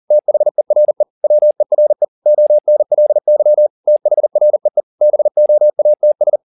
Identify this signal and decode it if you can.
Morse code, heard as a keyed tone, about 40 wpm, 600 Hz. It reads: THERE WERE ONLY THREE BOATS